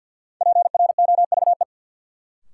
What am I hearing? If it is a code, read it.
Morse code, heard as a keyed tone, about 53 wpm, 700 Hz. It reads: PROVE
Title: Sentences from Top 1600 Words - New Word Drill Alt - 50wpm